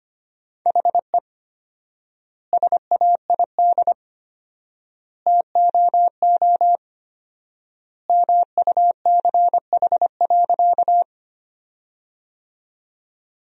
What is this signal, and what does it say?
Morse code, heard as a keyed tone, about 25 wpm, 700 Hz. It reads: HE SAID TOO MUCH.
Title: 4x Word-Spacing - Sentences from Top 200 Words - 25wpm